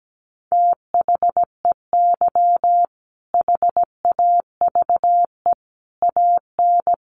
Morse code, heard as a keyed tone, about 17 wpm, 700 Hz. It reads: THEY HAVE AN
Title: Sentences from Top 1500 Words 17wpm